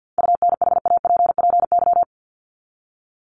Morse code, heard as a keyed tone, about 48 wpm, 700 Hz. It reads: UNHAPPY
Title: Top 1501-1750 Words 50wpm